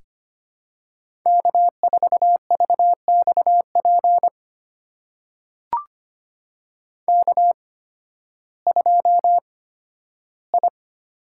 Morse code, heard as a keyed tone, about 25 wpm, spaced as for 13 wpm, 700 Hz. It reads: K4VXP K2I
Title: USA Call Signs - 5 Characters - 8x Character Spacing - 25wpm